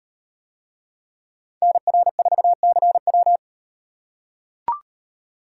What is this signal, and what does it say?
Morse code, heard as a keyed tone, about 38 wpm, 700 Hz. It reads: NR4CW